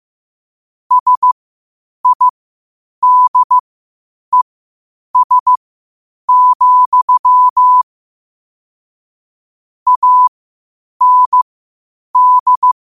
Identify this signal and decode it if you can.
Morse code, heard as a keyed tone, about 15 wpm, spaced as for 8 wpm, 1000 Hz. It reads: SIDES, AND
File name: qotdfiles / 8wpm